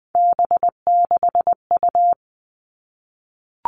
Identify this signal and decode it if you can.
Morse code, heard as a keyed tone, about 20 wpm, 700 Hz. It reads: B6U
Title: Three Challenging Letter-Number Combinations 20wpm